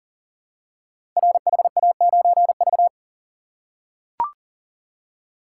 Morse code, heard as a keyed tone, about 39 wpm, 690 Hz. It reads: RHA9V
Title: Code Groups - 5 Letter:Numbers - 40wpm